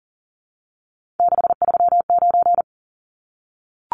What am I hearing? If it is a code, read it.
Morse code, heard as a keyed tone, about 40 wpm, 700 Hz. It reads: 639
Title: Numbers - 3-Digit 40wpm